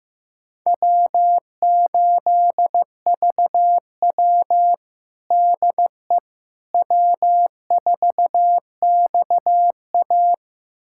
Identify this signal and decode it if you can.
Morse code, heard as a keyed tone, about 15 wpm, 700 Hz. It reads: W8VW DE W4XA
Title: USA Callsign DE Callsign - 4 Characters - 15wpm